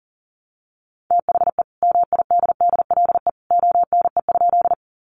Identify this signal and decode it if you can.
Morse code, heard as a keyed tone, about 40 wpm, 700 Hz. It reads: THE MIDDLE ONE?
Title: Sentences from Top 900 Words 40wpm